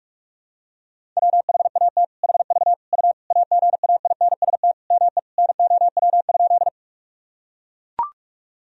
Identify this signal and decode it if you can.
Morse code, heard as a keyed tone, about 41 wpm, 700 Hz. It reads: WHAT HV U AGAINST ME NOW?
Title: Sentences from Top 400 Words and Abbreviations 45wpm